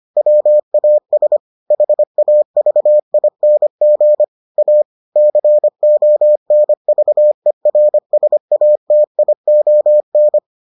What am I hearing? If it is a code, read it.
Morse code, heard as a keyed tone, about 25 wpm, 600 Hz. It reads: WAS HAVING A CONVERSATION